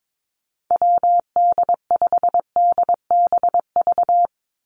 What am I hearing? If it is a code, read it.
Morse code, heard as a keyed tone, about 22 wpm, 700 Hz. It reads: WD5DBV
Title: Call Signs Repeated 2x - 22wpm